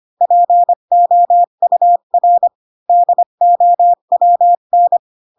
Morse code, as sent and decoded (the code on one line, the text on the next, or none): .--. --- ..- .-. / -.. --- .-- -.
POUR DOWN